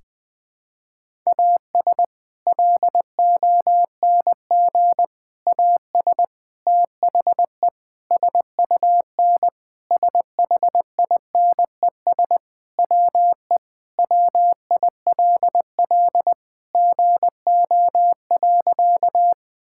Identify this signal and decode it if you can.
Morse code, heard as a keyed tone, about 21 wpm, 700 Hz. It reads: AS LONG AS THE SUN SHINES WE WILL GO.
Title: Sentences with 4-Word Phrases - 20wpm